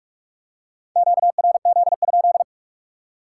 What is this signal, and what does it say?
Morse code, heard as a keyed tone, about 46 wpm, 700 Hz. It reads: QRZ?